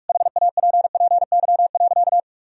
Morse code, heard as a keyed tone, about 46 wpm, 700 Hz. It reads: HAPPY.
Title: Sentences from Top 1000 Words - Review Part 1 - 45wpm